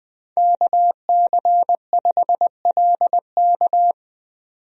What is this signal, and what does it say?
Morse code, heard as a keyed tone, about 21 wpm, 700 Hz. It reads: KC5LK
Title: USA Call Signs - 5 Characters - 20wpm